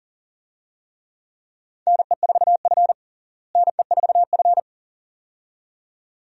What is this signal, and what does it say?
Morse code, heard as a keyed tone, about 37 wpm, 700 Hz. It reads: NE4F NE4F
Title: USA Call Signs - 4 Characters - Repeated 2x - 40wpm